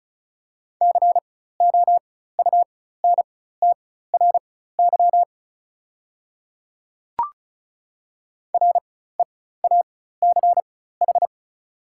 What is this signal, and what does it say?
Morse code, heard as a keyed tone, about 36 wpm, spaced as for 16 wpm, 690 Hz. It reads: COUNTRY REACH
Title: 4x Character Spacing - Top 500 Words - 35wpm